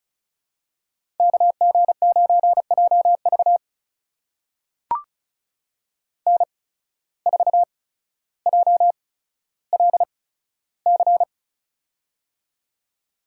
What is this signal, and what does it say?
Morse code, heard as a keyed tone, about 34 wpm, spaced as for 15 wpm, 700 Hz. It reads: KG9JV N4JLC
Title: USA Call Signs - 5 Characters - 8x Character Spacing - 35wpm